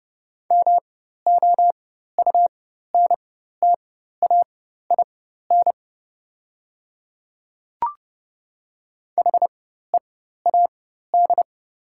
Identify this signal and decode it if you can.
Morse code, heard as a keyed tone, about 30 wpm, spaced as for 14 wpm, 700 Hz. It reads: MOUNTAIN HEAD
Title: 4x Character Spacing - Top 300 Words - 30wpm